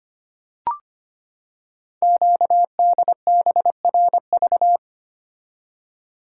Morse code, heard as a keyed tone, about 25 wpm, 700 Hz. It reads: QDBRV
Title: Code Groups - 5 Letters - 25wpm